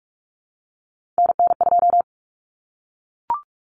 Morse code, heard as a keyed tone, about 44 wpm, 700 Hz. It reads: NN2
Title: Common Cut Numbers - 3-Digit 45wpm